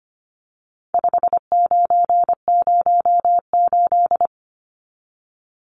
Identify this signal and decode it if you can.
Morse code, heard as a keyed tone, about 25 wpm, 700 Hz. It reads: 5908
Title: Numbers - 4-Digit 25wpm